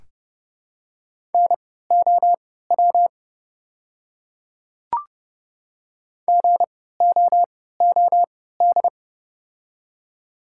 Morse code, heard as a keyed tone, about 32 wpm, spaced as for 17 wpm, 700 Hz. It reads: NOW GOOD
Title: 3x Character Spacing - Top 300 Words - 30wpm